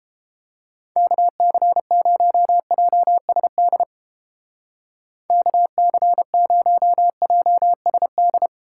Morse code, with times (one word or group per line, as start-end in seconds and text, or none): 1.0-3.8 KC0JSD
5.3-8.5 KC0JSD